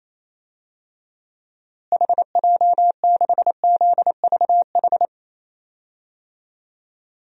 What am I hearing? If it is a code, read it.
Morse code, heard as a keyed tone, about 27 wpm, 700 Hz. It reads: HJ6ZVH
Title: DXCC-100 Call Signs - 6 Characters - Repeated 3x - 28wpm